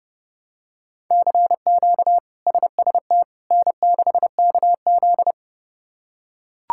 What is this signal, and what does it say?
Morse code, heard as a keyed tone, about 31 wpm, 700 Hz. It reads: CQ SST N6KZ
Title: K1USN SST QSO Practice - 30wpm